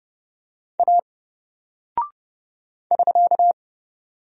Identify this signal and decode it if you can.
Morse code, heard as a keyed tone, about 30 wpm, 700 Hz. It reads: A <SK>
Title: Single Character (Letter:Number:Punctuation:Prosign) 30wpm